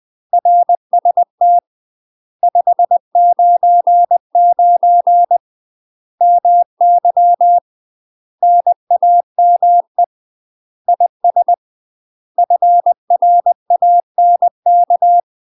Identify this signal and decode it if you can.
Morse code, heard as a keyed tone, about 20 wpm, 700 Hz. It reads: RST 599 MY NAME IS FRANK
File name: UK MASTER QSOs TRAINING / 20WPM / UK MASTER QSOs 0042